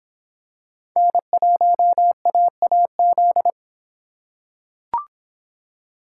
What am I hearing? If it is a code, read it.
Morse code, heard as a keyed tone, about 26 wpm, 700 Hz. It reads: N1AAZ